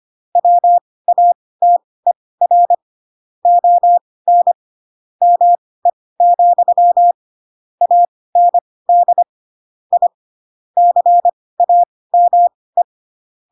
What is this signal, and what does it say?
Morse code, heard as a keyed tone, about 25 wpm, spaced as for 18 wpm, 700 Hz. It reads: WATER ON ME, AND I CAME
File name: Pinocchio / Pinocchio ch7 12 25 ik1hge